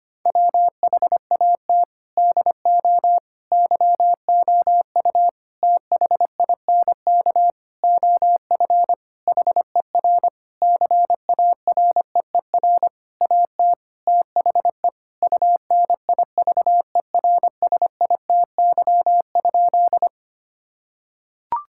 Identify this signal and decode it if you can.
Morse code, heard as a keyed tone, about 25 wpm, 700 Hz. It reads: WHAT DO YOU THINK OF HER CAREER AT THE UNIVERSITY?
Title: Sentences from Top 1000 Words - Review Part 6 - 25wpm